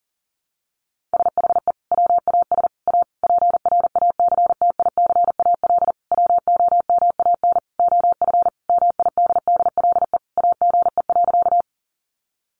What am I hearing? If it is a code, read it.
Morse code, heard as a keyed tone, about 40 wpm, 700 Hz. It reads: SHE WAS A PRACTICAL WOMAN OF MIDDLE AGE.
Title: Sentences from Top 1400 Words 40wpm